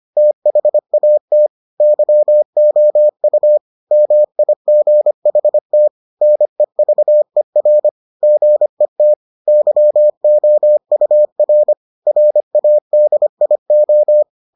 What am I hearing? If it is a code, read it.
Morse code, heard as a keyed tone, about 25 wpm, 600 Hz. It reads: THAT YOU MIGHT NEVER GET YOUR RADIO